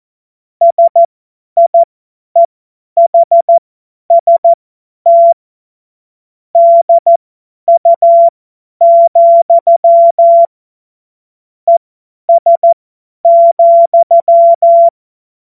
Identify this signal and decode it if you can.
Morse code, heard as a keyed tone, about 14 wpm, spaced as for 10 wpm, 680 Hz. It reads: SIEHST DU, ES,
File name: Morsekurs / Witze / Witz 16. Dezember 2025 50 F70